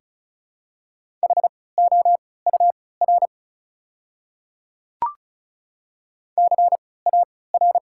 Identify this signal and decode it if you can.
Morse code, heard as a keyed tone, about 36 wpm, spaced as for 20 wpm, 700 Hz. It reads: HOUR CAR
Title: 3x Character Spacing - Top 500 Words - 35wpm